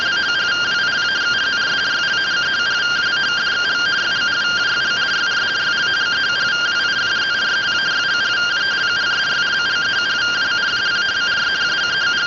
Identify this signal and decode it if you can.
Two-tone teleprinter signal, about 170 Hz shift.